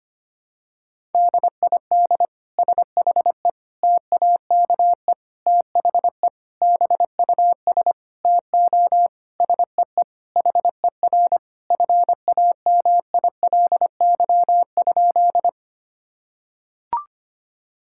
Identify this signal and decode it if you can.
Morse code, heard as a keyed tone, about 25 wpm, 700 Hz. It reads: DID SHE TAKE THE BUS TO SEE HER FAMILY?
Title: Sentences from Top 1000 Words - Review Part 6 - 25wpm